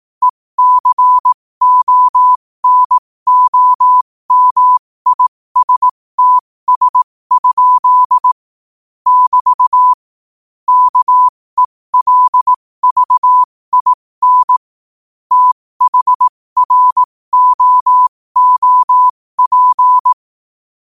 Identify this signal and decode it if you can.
Morse code, heard as a keyed tone, about 18 wpm, spaced as for 15 wpm, 1000 Hz. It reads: ECONOMISTS? <BT> KELVIN THROOP
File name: qotdfiles / 15wpm